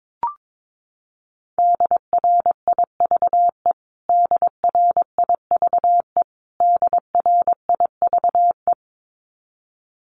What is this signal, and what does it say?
Morse code, heard as a keyed tone, about 22 wpm, 700 Hz. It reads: DRIVE DRIVE DRIVE
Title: Top 500 Words Repeated 22wpm